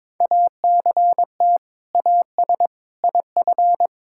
Morse code, heard as a keyed tone, about 22 wpm, 700 Hz. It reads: ACT AS IF
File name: practice / 22wpm / Sentences with 2-Word Phrases - 22wpm